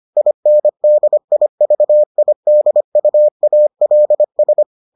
Morse code, heard as a keyed tone, about 25 wpm, 600 Hz. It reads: INDIVIDUALS